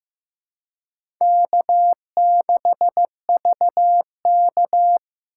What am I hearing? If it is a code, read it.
Morse code, heard as a keyed tone, about 15 wpm, 700 Hz. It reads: K6VK
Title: USA Call Signs - 4 Characters - 4x Character Spacing - 15wpm